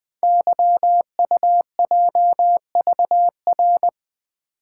Morse code, heard as a keyed tone, about 20 wpm, 700 Hz. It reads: YUJVR
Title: Code Groups - 5 Letters - 20wpm